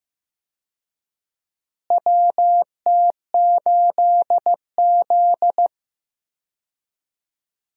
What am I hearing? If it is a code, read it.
Morse code, heard as a keyed tone, about 16 wpm, 700 Hz. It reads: WT8Z
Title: USA Call Signs - 4 Characters - Repeated 3x - 15wpm